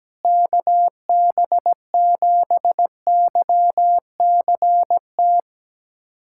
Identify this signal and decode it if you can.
Morse code, heard as a keyed tone, about 17 wpm, 700 Hz. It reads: KB7YCT
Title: USA Callsign DE Callsign 17wpm